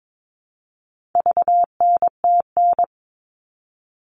Morse code, heard as a keyed tone, about 22 wpm, 700 Hz. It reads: VNTN
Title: All Cut Numbers - 4-Digit 22wpm